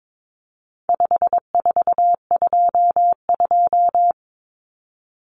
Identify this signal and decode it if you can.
Morse code, heard as a keyed tone, about 22 wpm, 700 Hz. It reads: 5422
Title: Numbers - 4-Digit 22wpm